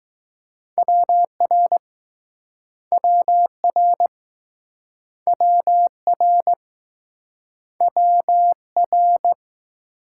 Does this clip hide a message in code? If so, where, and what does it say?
Yes: Morse code from 0.8 to 9.3 s, heard as a keyed tone, about 20 wpm, 700 Hz. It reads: WR WR WR WR